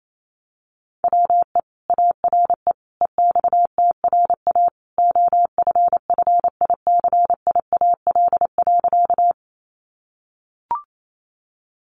Morse code, heard as a keyed tone, about 28 wpm, 700 Hz. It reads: WE ARE EXTRA OFFICIAL.
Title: Sentences from Top 1000 Words 28wpm